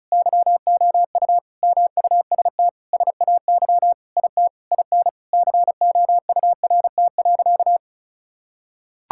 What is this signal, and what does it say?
Morse code, heard as a keyed tone, about 36 wpm, 700 Hz. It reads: YOU MUST SAY IT IN COURT.
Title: Sentences from Top 1600 Words 35wpm